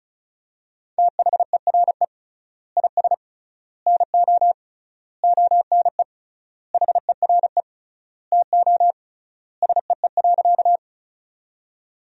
Morse code, heard as a keyed tone, about 34 wpm, 700 Hz. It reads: THERE IS NO ONE HERE TO SEE.
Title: 3x Word-Spacing - Sentences from Top 200 Words - 35wpm